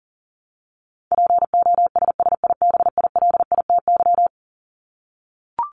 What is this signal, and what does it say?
Morse code, heard as a keyed tone, about 40 wpm, 700 Hz. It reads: POSSIBILITY
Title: Top 1751-2000 Words 40wpm